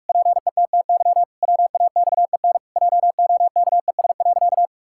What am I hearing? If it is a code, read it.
Morse code, heard as a keyed tone, about 46 wpm, 700 Hz. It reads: PETTY WAXEN JOKES.
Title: Pangram Challenge v2 - 45wpm